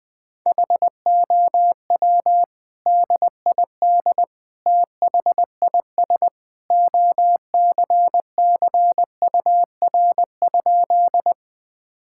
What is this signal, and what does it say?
Morse code, heard as a keyed tone, about 21 wpm, 700 Hz. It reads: HOW DID THIS OCCUR?
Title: Sentences from Top 1000 Words - Review Part 6 - 20wpm